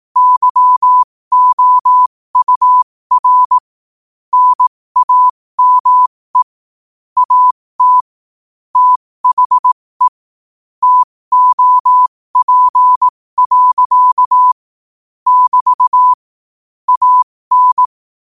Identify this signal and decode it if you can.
Morse code, heard as a keyed tone, about 18 wpm, spaced as for 15 wpm, 1000 Hz. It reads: YOUR NAME AT THE TOP. <BT> AN
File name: qotdfiles / 15wpm